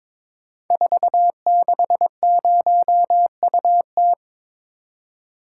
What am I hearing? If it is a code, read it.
Morse code, heard as a keyed tone, about 22 wpm, 700 Hz. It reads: 460UT
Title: Code Groups - 5 Letter:Numbers - 22wpm